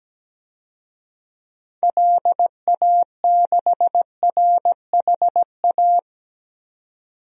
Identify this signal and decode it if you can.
Morse code, heard as a keyed tone, about 17 wpm, 700 Hz. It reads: LA6RHA
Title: DXCC-100 Call Signs - 6 Characters - 17wpm